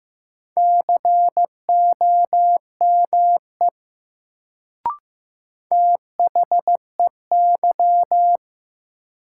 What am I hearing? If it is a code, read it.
Morse code, heard as a keyed tone, about 15 wpm, 700 Hz. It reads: COME THEY
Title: Sentences from Top 100 Words - New Word Drill - 15wpm